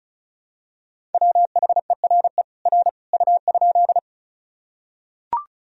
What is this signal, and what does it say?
Morse code, heard as a keyed tone, about 34 wpm, 700 Hz. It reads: WHERE R U?
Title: Sentences from Top 200 Words and Abbreviations 35wpm